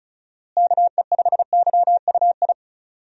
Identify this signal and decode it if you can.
Morse code, heard as a keyed tone, about 36 wpm, 700 Hz. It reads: KE5YUI
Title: USA Call Signs - 6 Characters - Repeated 2x - 35wpm